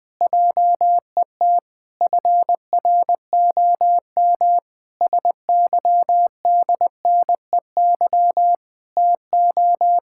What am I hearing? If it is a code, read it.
Morse code, heard as a keyed tone, about 19 wpm, 700 Hz. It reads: JET FROM SYDNEY TO
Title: Pangram Challenge v2 - 20wpm